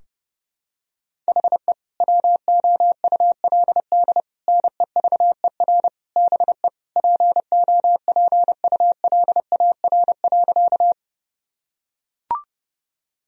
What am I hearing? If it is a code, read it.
Morse code, heard as a keyed tone, about 32 wpm, 700 Hz. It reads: HE WOULD NEVER BE POPULAR.
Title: Sentences from Top 1000 Words - Review Part 3 - 30wpm